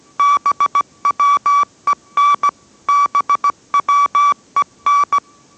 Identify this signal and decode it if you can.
Morse code, heard as a keyed tone, about 17 wpm, 1150 Hz. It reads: BWEN BWEN